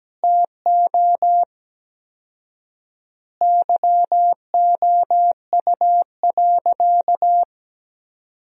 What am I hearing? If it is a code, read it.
Morse code, heard as a keyed tone, about 17 wpm, 700 Hz. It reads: TO YOU.
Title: 4x Word-Spacing - Sentences from Top 200 Words - 17wpm